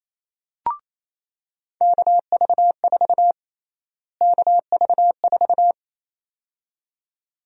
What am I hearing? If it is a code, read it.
Morse code, heard as a keyed tone, about 28 wpm, 700 Hz. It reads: KV4 KV4
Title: USA Call Sign Prefixes - Repeated 2x - 28wpm